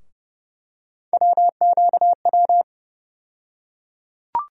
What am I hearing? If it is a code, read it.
Morse code, heard as a keyed tone, about 30 wpm, 700 Hz. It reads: WQW
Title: USA Call Sign Postfixes - 3 Letters 30wpm